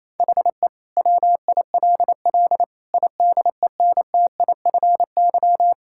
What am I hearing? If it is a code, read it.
Morse code, heard as a keyed tone, about 28 wpm, 700 Hz. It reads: HE WILL IDENTIFY
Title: Sentences from Top 1000 Words - Review Part 6 - 28wpm